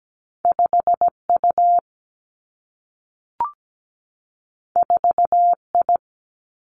Morse code, heard as a keyed tone, about 17 wpm, 700 Hz. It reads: 5U 4I
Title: Two Challenging Letter-Number Combinations 17wpm